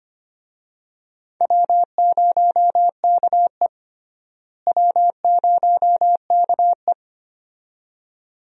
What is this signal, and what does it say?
Morse code, heard as a keyed tone, about 25 wpm, 700 Hz. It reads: W0KE W0KE
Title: USA Call Signs - 4 Characters - Repeated 2x - 25wpm